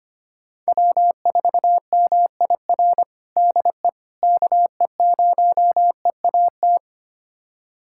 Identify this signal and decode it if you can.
Morse code, heard as a keyed tone, about 25 wpm, 700 Hz. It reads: W4MIR DE KE0EAT
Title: USA Callsign DE Callsign 25wpm